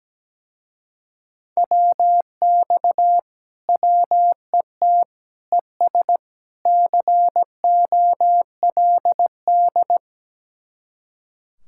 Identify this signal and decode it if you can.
Morse code, heard as a keyed tone, about 17 wpm, 700 Hz. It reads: WX WET ES COLD
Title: Speed-Racing Common QSO Phrases - 17wpm